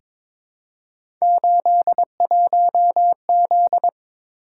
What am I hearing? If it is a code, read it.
Morse code, heard as a keyed tone, about 22 wpm, 700 Hz. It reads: 81Z
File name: practice / 22wpm / Three Letter-Number Combinations - 22wpm